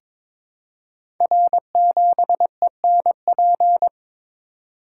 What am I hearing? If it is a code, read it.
Morse code, heard as a keyed tone, about 22 wpm, 690 Hz. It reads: R7ENP
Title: Code Groups - 5 Letter:Numbers - 22wpm